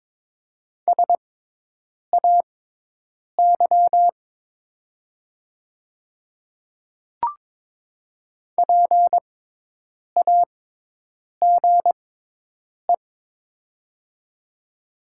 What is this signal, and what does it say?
Morse code, heard as a keyed tone, about 22 wpm, spaced as for 8 wpm, 700 Hz. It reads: SAY PAGE
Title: 6x Character Spacing - Top 500 Words - 22wpm